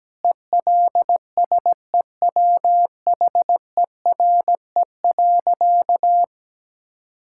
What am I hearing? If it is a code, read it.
Morse code, heard as a keyed tone, about 17 wpm, 700 Hz. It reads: ELSEWHERE.
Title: Sentences from Top 1600 Words 17wpm